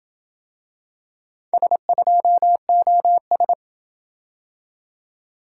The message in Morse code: S2OS

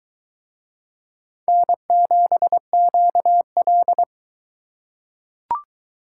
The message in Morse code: N7QL

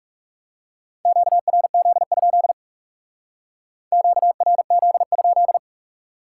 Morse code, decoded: QRZ? QRZ?

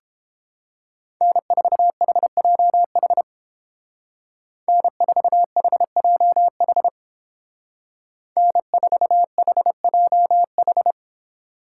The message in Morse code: N4HJH N4HJH N4HJH

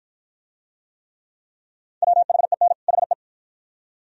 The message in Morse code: WHEN HE

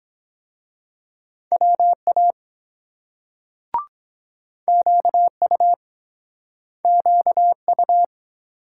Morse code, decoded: WA QU QU

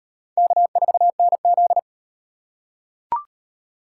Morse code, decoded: K4NZ

